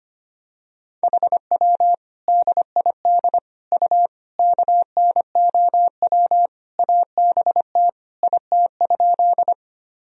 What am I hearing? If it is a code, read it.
Morse code, heard as a keyed tone, about 25 wpm, 700 Hz. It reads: HW DID U KNOW ABT IT?
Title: Sentences from Top 100 Words and Abbreviations 25wpm